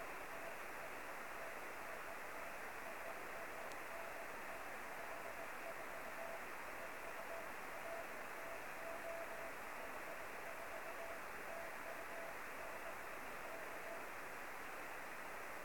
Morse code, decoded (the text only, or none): none